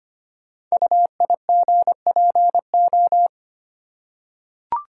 Morse code, heard as a keyed tone, about 25 wpm, 700 Hz. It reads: UIGPO
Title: Code Groups - 5 Letters - 25wpm